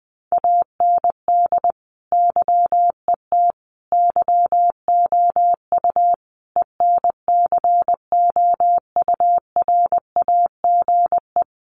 Morse code, heard as a keyed tone, about 20 wpm, 700 Hz. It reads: AND YET YOU ENCOURAGE
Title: Sentences from Top 1400 Words 20wpm